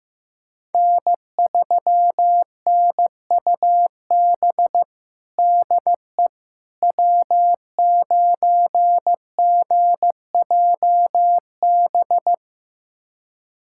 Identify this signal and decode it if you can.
Morse code, heard as a keyed tone, about 15 wpm, 700 Hz. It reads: N3NUB DE W9GJB
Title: USA Callsign DE Callsign - 5 Characters - 15wpm